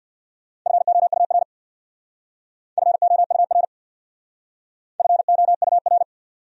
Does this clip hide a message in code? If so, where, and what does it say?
Yes: Morse code from 0.7 to 6.0 s, heard as a keyed tone, about 64 wpm, 700 Hz. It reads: FOUR FOUR FOUR